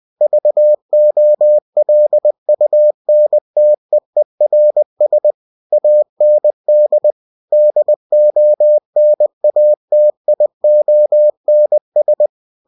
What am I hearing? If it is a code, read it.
Morse code, heard as a keyed tone, about 20 wpm, 600 Hz. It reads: VOLUNTEERS AND DONATIONS
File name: cw-books / winnie-the-pooh / winnie-20wpm-page-0120